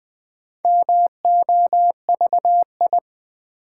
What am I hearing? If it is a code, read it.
Morse code, heard as a keyed tone, about 20 wpm, 700 Hz. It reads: MOVI
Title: Build A Word - 2 Letters At A Time - 6-Letter Words - 20wpm